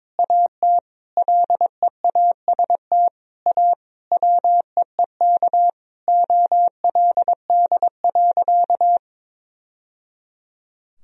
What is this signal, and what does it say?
Morse code, heard as a keyed tone, about 22 wpm, 700 Hz. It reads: AT LEAST A WEEK OLD.
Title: Sentences from Top 1000 Words - Review Part 2 - 22wpm